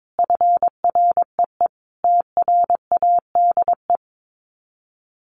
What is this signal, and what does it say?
Morse code, heard as a keyed tone, about 22 wpm, 700 Hz. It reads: FREE TRADE
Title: Sets of 5 Words Encore 22wpm